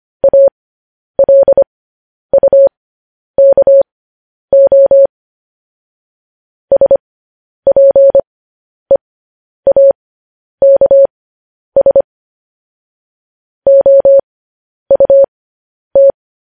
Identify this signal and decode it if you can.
Morse code, heard as a keyed tone, about 25 wpm, spaced as for 10 wpm, 550 Hz. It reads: ALUKO SPEAKS OUT
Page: news headlines in morse code at 10 wpm